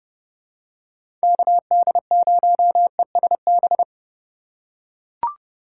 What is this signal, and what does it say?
Morse code, heard as a keyed tone, about 31 wpm, 700 Hz. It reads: KD0ESB
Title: USA Call Signs - 6 Characters - 4x Character Spacing - 30wpm